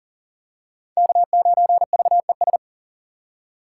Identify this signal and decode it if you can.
Morse code, heard as a keyed tone, about 37 wpm, 700 Hz. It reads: K9VES
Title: USA Call Signs - 5 Characters - 6x Character Spacing - 40wpm